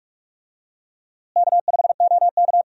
Morse code, heard as a keyed tone, about 42 wpm, 700 Hz. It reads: K5OK
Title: USA Call Signs - 4 Characters - Repeated 2x - 45wpm